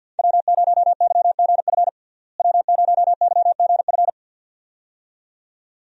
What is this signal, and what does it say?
Morse code, heard as a keyed tone, about 50 wpm, 700 Hz. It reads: W0YGF W0YGF
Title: USA Call Signs - 5 Characters - Repeated 2x - 50wpm